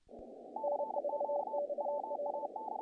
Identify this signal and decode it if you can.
Morse code, heard as a keyed tone, about 27 wpm, 530 Hz, with noise. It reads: N4XL
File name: contests / naqp-cw-k5tr-op-k5pi- -1800